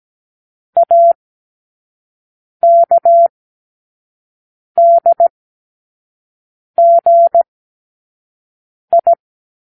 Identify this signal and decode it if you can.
Morse code, heard as a keyed tone, about 17 wpm, spaced as for 5 wpm, 680 Hz. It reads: AKDGI
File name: k7qo / Lesson18(A-L)